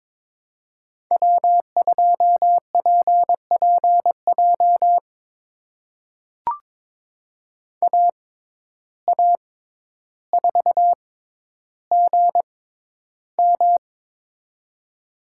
Morse code, heard as a keyed tone, about 22 wpm, spaced as for 11 wpm, 700 Hz. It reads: W2PPJ AA4GM
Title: USA Call Signs - 5 Characters - 6x Character Spacing - 22wpm